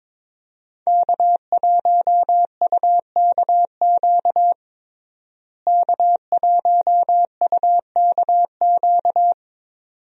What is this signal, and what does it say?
Morse code, heard as a keyed tone, about 22 wpm, 700 Hz. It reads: K1UKQ K1UKQ
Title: USA Call Signs - 5 Characters - Repeated 2x - 22wpm